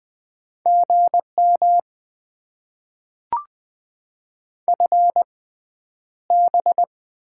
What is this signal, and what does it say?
Morse code, heard as a keyed tone, about 20 wpm, spaced as for 10 wpm, 690 Hz. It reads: GM FB
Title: Top 20 QSO Elements - 6x Character Spacing - 20wpm